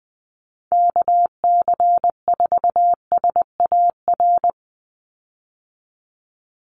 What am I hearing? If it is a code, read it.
Morse code, heard as a keyed tone, about 20 wpm, 700 Hz. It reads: KC4SAR
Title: USA Call Signs 20wpm